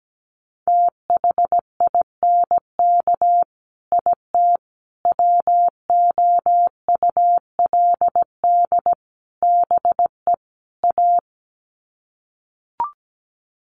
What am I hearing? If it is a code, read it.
Morse code, heard as a keyed tone, about 17 wpm, 700 Hz. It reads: THINK IT WOULD BE A
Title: Sets of 5 Words Encore2 17wpm